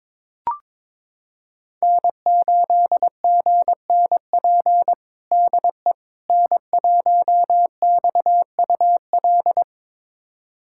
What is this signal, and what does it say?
Morse code, heard as a keyed tone, about 21 wpm, 700 Hz. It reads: N8GNP DE N1XUL
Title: USA Callsign DE Callsign - 5 Characters - 22wpm